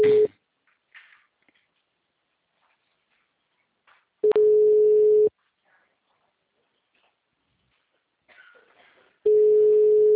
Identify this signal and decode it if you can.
Ringback tone (Europe, 425 Hz).
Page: Itu